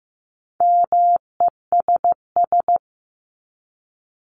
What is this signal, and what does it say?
Morse code, heard as a keyed tone, about 15 wpm, 700 Hz. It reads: MESS
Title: Top 1751-2000 Words 15wpm